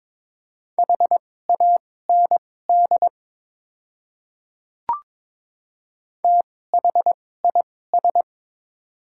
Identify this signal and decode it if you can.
Morse code, heard as a keyed tone, about 21 wpm, spaced as for 16 wpm, 700 Hz. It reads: HAND THIS
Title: 2x Character Spacing - Top 200 Words - 22wpm